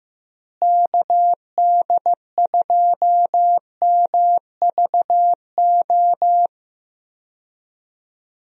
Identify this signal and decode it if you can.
Morse code, heard as a keyed tone, about 16 wpm, 700 Hz. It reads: KD2MVO
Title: USA Call Signs - 6 Characters - Repeated 2x - 15wpm